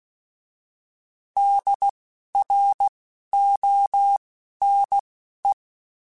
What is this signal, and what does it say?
Morse code, heard as a keyed tone, about 16 wpm, spaced as for 12 wpm, 800 Hz. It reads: DRONE
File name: Alphabetical / Alphabetical mp3s / Ds @10